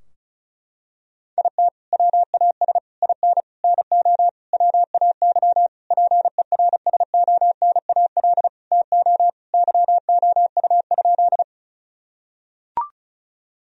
Morse code, heard as a keyed tone, about 34 wpm, 700 Hz. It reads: IT WAS IN NO WAY PERSONAL TO YOU?
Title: Sentences from Top 1000 Words - Review Part 4 - 35wpm